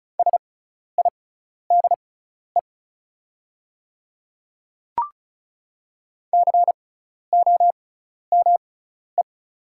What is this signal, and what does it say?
Morse code, heard as a keyed tone, about 36 wpm, spaced as for 12 wpm, 700 Hz. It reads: SIDE COME
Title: 6x Character Spacing - Top 100 Words - 35wpm